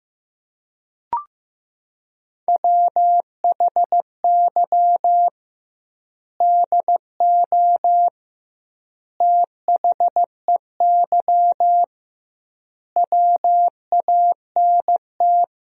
Morse code, heard as a keyed tone, about 15 wpm, 700 Hz. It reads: WHY DO THEY WANT
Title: 2x Word-Spacing - Sentences from Top 400 Words - 15wpm